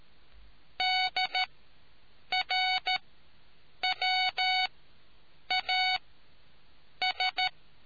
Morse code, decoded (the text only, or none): DRWAS